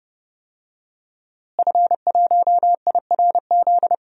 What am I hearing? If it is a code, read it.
Morse code, heard as a keyed tone, about 31 wpm, 700 Hz. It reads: F1IRZ
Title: DXCC-100 Call Signs - 5 Characters - Repeated 3x - 30wpm